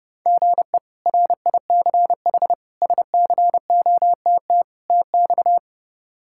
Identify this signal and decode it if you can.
Morse code, heard as a keyed tone, about 32 wpm, 700 Hz. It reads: GE RICH SCOTT TX
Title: K1USN SST QSO Practice - 30wpm